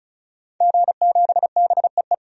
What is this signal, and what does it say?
Morse code, heard as a keyed tone, about 36 wpm, 700 Hz. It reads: G7BEE
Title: DXCC-100 Call Signs - 5 Characters - 35wpm